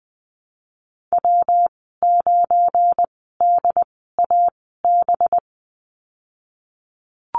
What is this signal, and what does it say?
Morse code, heard as a keyed tone, about 20 wpm, spaced as for 14 wpm, 700 Hz. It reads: W9DAB
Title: 193 Lesson 31g - Letter v - Call Signs 20wpm